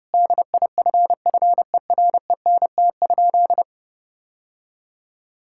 Morse code, heard as a keyed tone, about 30 wpm, 700 Hz. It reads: DIFFERENT?
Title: Sentences from Top 1600 Words 30wpm